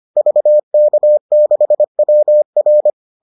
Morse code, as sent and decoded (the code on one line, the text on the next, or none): ...- -.- -.... .-- .-.
VK6WR